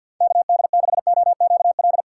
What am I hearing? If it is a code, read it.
Morse code, heard as a keyed tone, about 48 wpm, 700 Hz. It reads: KD6OQL
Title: USA Call Signs - 6 Characters - 6x Character Spacing - 50wpm